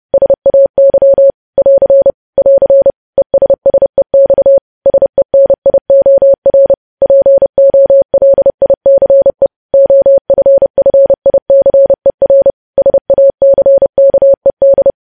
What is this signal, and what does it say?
Morse code, heard as a keyed tone, about 30 wpm, 550 Hz. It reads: SAY <AR> <AR> ESSEX SENIOR POLICE OFFICER SACKED
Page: News Headlines in Morse Code at 30 WPM